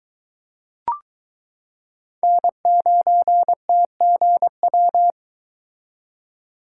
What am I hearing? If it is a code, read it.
Morse code, heard as a keyed tone, about 22 wpm, 700 Hz. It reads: N9TGW